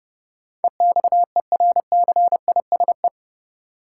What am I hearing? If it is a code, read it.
Morse code, heard as a keyed tone, about 32 wpm, 700 Hz. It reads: EXERCISE
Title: Sentences from Top 1200 Words - New Word Drill Alt - 30wpm